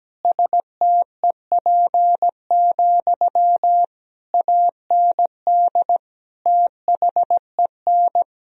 Morse code, heard as a keyed tone, about 17 wpm, 700 Hz. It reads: STEP, AND THEN
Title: Sentences from Top 1000 Words - Review Part 2 - 17wpm